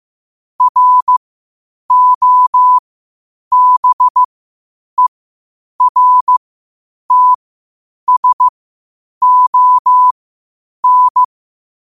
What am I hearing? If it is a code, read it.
Morse code, heard as a keyed tone, about 15 wpm, spaced as for 8 wpm, 1000 Hz. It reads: ROBERTSON
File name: qotdfiles / 8wpm